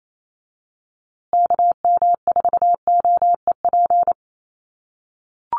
Morse code, heard as a keyed tone, about 28 wpm, 700 Hz. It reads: KM4OEP
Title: USA Call Signs Repeated 3x 28wpm